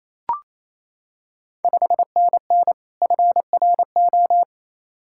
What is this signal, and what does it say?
Morse code, heard as a keyed tone, about 28 wpm, 700 Hz. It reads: 5NN FRO
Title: Ontario QSO Party Exchange 28wpm